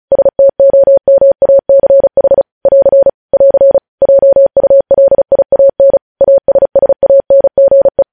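Morse code, heard as a keyed tone, about 36 wpm, 550 Hz. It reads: STOMACH <AR> <AR> JULIAN ASSANGE
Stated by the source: Morse code practice of news headlines at 35 wpm, opening with vvv.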